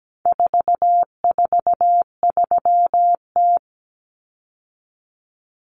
Morse code, heard as a keyed tone, about 17 wpm, 700 Hz. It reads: 443T
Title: Common Cut Numbers - 4-Digit 17wpm